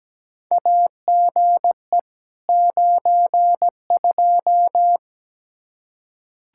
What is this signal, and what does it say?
Morse code, heard as a keyed tone, about 17 wpm, 700 Hz. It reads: AGE 92
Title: Practice QSO - Exchange 3 - More Info - 17wpm